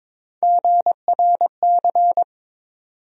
Morse code, heard as a keyed tone, about 22 wpm, 700 Hz. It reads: GRC
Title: USA Call Sign Postfixes - 3 Letters 22wpm